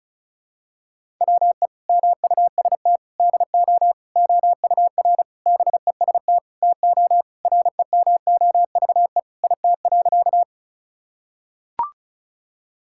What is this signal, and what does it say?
Morse code, heard as a keyed tone, about 36 wpm, 700 Hz. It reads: WE MUST DO OUR BEST TO REMOVE IT.
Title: Sentences from Top 1000 Words - Review Part 5 - 35wpm